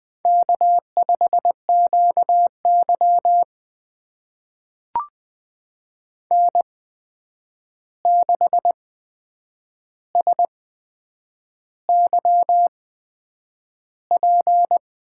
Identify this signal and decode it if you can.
Morse code, heard as a keyed tone, about 20 wpm, spaced as for 8 wpm, 700 Hz. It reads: K5QY N6SYP
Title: USA Call Signs - 8x Character Spacing - 20wpm